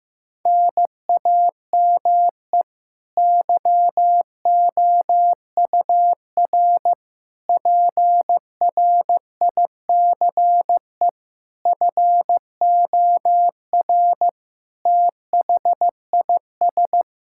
Morse code, heard as a keyed tone, about 15 wpm, 700 Hz. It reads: NAME YOUR PRICE FOR THIS
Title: Sentences from Top 1000 Words - Review Part 5 - 15wpm